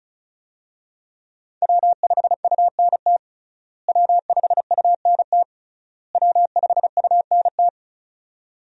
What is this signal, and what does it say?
Morse code, heard as a keyed tone, about 36 wpm, 700 Hz. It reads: W5UNT W5UNT W5UNT
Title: USA Call Signs - 5 Characters - Repeated 3x - 35wpm